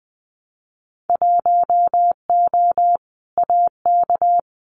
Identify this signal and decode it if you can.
Morse code, heard as a keyed tone, about 20 wpm, 700 Hz. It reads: 1O AK
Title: Winter Field Day Exchanges Encore 20wpm